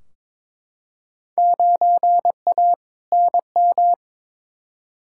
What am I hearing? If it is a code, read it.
Morse code, heard as a keyed tone, about 22 wpm, 700 Hz. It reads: 9A NM
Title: Field Day Exchanges Encore - 22wpm